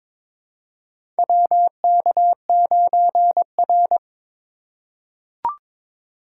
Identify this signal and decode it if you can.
Morse code, heard as a keyed tone, about 21 wpm, 700 Hz. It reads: WK9R
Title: USA Call Signs - 4 Characters - 3x Character Spacing - 22wpm